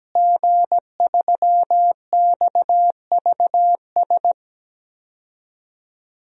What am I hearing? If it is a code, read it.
Morse code, heard as a keyed tone, about 17 wpm, 700 Hz. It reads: G3XVS
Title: DXCC-100 Call Signs - 5 Characters - Repeated 2x - 17wpm